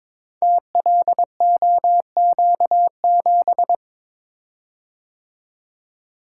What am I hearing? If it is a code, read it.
Morse code, heard as a keyed tone, about 22 wpm, 690 Hz. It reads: TLOQ7
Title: Code Groups - 5 Letter:Numbers - 22wpm